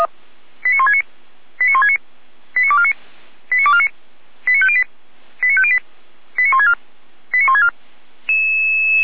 Five-tone selective call (ZVEI), heard as quick runs of five tones, tones of about 70 ms. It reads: 89169 89169 89249 89249 89508 89508 89164 89164
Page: SDR/ZVEI